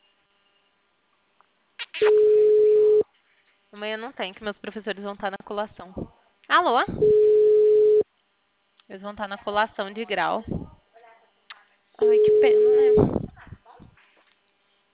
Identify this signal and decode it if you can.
Ringback tone (Europe, 425 Hz), with speech.
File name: out- -2028- - - .57882